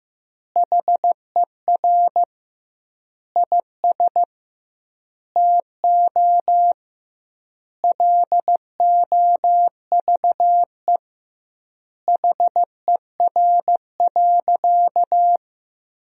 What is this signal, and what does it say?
Morse code, heard as a keyed tone, about 16 wpm, 700 Hz. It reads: HER IS TO LOVE HER.
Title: 2x Word-Spacing - Sentences from Top 400 Words - 15wpm